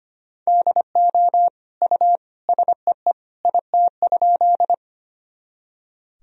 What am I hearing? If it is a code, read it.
Morse code, heard as a keyed tone, about 25 wpm, 700 Hz. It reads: DO U SEE IT?
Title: Sentences from Top 100 Words and Abbreviations 25wpm